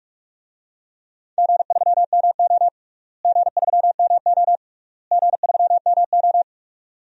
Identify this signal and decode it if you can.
Morse code, heard as a keyed tone, about 41 wpm, 700 Hz. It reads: G3MO G3MO G3MO
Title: DXCC-100 Call Signs - 4 Characters - Repeated 3x - 45wpm